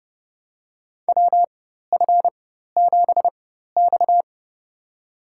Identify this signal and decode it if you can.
Morse code, heard as a keyed tone, about 31 wpm, spaced as for 14 wpm, 700 Hz. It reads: WF7X
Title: USA Call Signs - 4 Characters - 4x Character Spacing - 30wpm